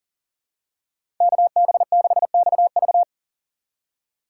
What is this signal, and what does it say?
Morse code, heard as a keyed tone, about 40 wpm, 700 Hz. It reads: KB6XV
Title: USA Call Signs - 3x Character Spacing - 40wpm